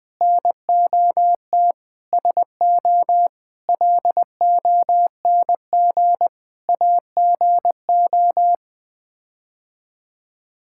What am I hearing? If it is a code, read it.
Morse code, heard as a keyed tone, about 20 wpm, 700 Hz. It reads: NOT SO LONG AGO
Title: Sets of 8 Words - 20wpm